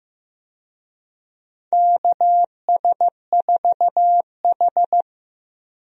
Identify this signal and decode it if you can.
Morse code, heard as a keyed tone, about 15 wpm, 700 Hz. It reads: KS4H
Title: USA Call Signs - 4 Characters - Repeated 3x - 15wpm